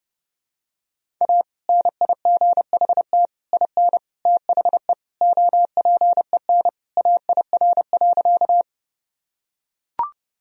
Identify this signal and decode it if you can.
Morse code, heard as a keyed tone, about 30 wpm, 700 Hz. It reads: A NIGHT IN THE OPEN AIR.
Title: Sentences from Top 1000 Words - Review Part 1 - 30wpm